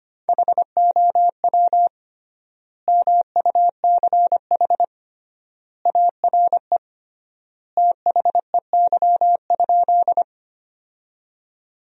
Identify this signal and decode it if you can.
Morse code, heard as a keyed tone, about 25 wpm, 700 Hz. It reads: HOW MUCH ARE THEY?
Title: 3x Word-Spacing - Sentences from Top 200 Words - 25wpm